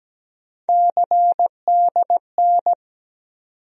Morse code, heard as a keyed tone, about 17 wpm, 700 Hz. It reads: CDN
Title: USA Call Sign Postfixes - 3 Letters 17wpm